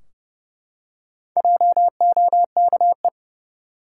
Morse code, heard as a keyed tone, about 30 wpm, 700 Hz. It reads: JOKE
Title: Four Letter Words - 30wpm